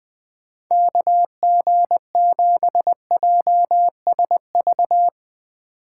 Morse code, heard as a keyed tone, about 22 wpm, 700 Hz. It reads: KG7JSV
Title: Speed Racing - USA 2x3 Call Signs 20wpm